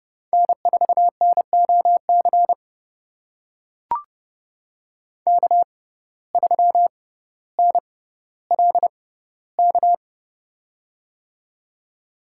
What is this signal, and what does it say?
Morse code, heard as a keyed tone, about 30 wpm, spaced as for 15 wpm, 700 Hz. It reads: N4NOC K3NLK
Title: USA Call Signs - 5 Characters - 6x Character Spacing - 30wpm